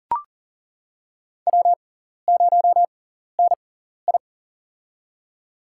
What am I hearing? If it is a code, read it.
Morse code, heard as a keyed tone, about 40 wpm, spaced as for 14 wpm, 700 Hz. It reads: W0NI
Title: USA Call Signs - 4 Characters - 6x Character Spacing - 40wpm